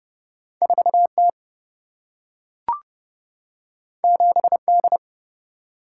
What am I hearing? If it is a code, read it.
Morse code, heard as a keyed tone, about 30 wpm, 700 Hz. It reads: 4 T 7 D
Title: Single-Letter Single-Number 30wpm